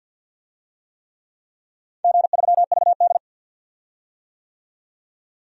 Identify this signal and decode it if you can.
Morse code, heard as a keyed tone, about 52 wpm, 700 Hz. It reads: G3VD
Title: DXCC-100 Call Signs - 4 Characters - 50wpm